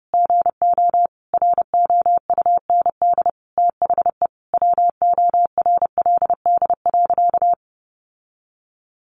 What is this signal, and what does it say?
Morse code, heard as a keyed tone, about 30 wpm, 700 Hz. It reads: GO ROUND THE WORLD.
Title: Sentences from Top 600 Words 30wpm